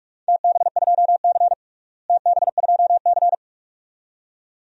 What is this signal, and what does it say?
Morse code, heard as a keyed tone, about 46 wpm, 700 Hz. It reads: TB2C TB2C
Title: DXCC-100 Call Signs - 4 Characters - Repeated 2x - 45wpm